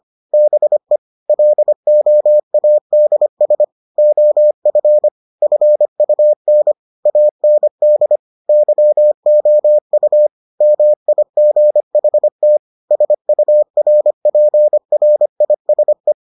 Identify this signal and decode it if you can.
Morse code, heard as a keyed tone, about 25 wpm, 600 Hz. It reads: BE LOADS OF FUN AND YOU MIGHT SURPRISE